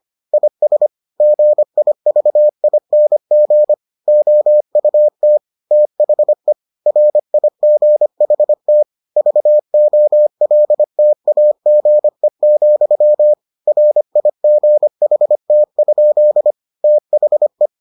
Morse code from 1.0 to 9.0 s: GIVING OUT THE RIGHT

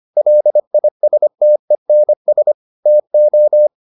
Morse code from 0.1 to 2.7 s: LISTENS

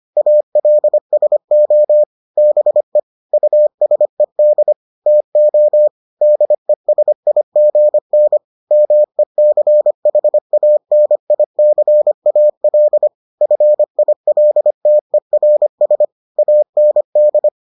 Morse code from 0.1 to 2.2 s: ALSO

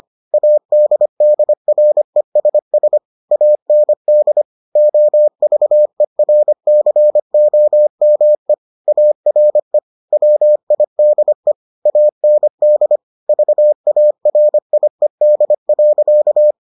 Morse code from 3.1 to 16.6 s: AND OVERCOME ARE WIDE AND VARIED.

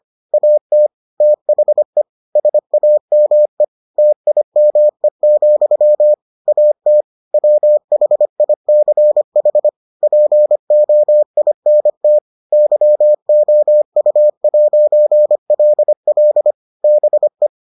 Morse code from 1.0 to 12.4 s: THE SAME TIME, AT WHICH POINT

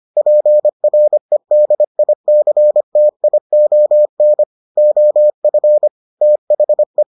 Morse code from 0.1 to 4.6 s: PREDICTION

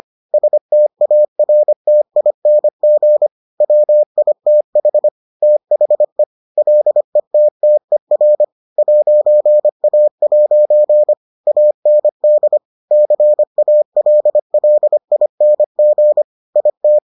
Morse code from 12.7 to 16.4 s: CALLING

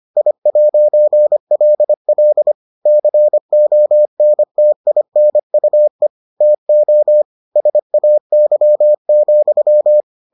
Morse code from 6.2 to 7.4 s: TO

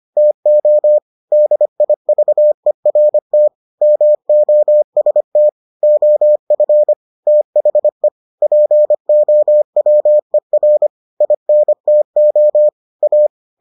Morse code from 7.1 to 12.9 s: THE POWER INTO

